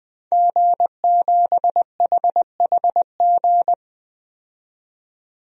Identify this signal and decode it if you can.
Morse code, heard as a keyed tone, about 20 wpm, 700 Hz. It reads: G7HHG
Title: DXCC-100 Call Signs - 5 Characters - 20wpm